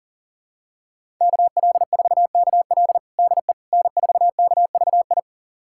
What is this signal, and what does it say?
Morse code, heard as a keyed tone, about 40 wpm, 700 Hz. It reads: KL4KL DE N4KVI
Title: USA Callsign DE Callsign - 5 Characters - 40wpm